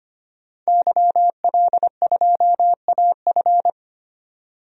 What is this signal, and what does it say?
Morse code, heard as a keyed tone, about 25 wpm, 700 Hz. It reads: YL2AF
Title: DXCC-100 Call Signs - 5 Characters - 25wpm